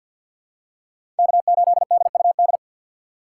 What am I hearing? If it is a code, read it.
Morse code, heard as a keyed tone, about 47 wpm, 700 Hz. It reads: K8DUD
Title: USA Call Signs - 5 Characters - 3x Character Spacing - 50wpm